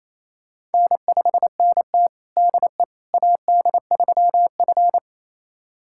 Morse code, heard as a keyed tone, about 28 wpm, 700 Hz. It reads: N5NT DE AD3F
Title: USA Callsign DE Callsign - 4 Characters - 28wpm